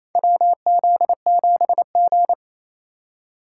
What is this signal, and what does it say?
Morse code, heard as a keyed tone, about 28 wpm, 700 Hz. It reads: WZ7G
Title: USA Call Signs - 4x Character Spacing - 28wpm